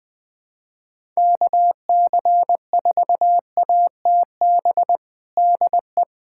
Morse code, heard as a keyed tone, about 21 wpm, 700 Hz. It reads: KC4ATB DE
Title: USA Callsign DE Callsign - 6 Characters - 20wpm